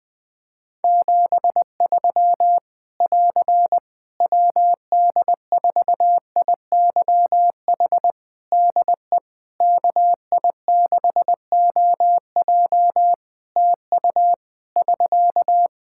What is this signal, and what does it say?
Morse code, heard as a keyed tone, about 21 wpm, 700 Hz. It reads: 73 <AR> WD4IYH DE KI6OJ TU <SK>
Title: K1USN SST QSO Practice 20wpm